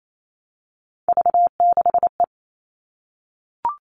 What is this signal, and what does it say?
Morse code, heard as a keyed tone, about 28 wpm, 700 Hz. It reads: V6E
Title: All Cut Numbers - 3-Digit 28wpm